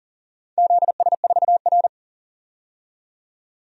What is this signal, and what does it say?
Morse code, heard as a keyed tone, about 37 wpm, 700 Hz. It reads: ZS4R
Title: DXCC-100 Call Signs - 4 Characters - 40wpm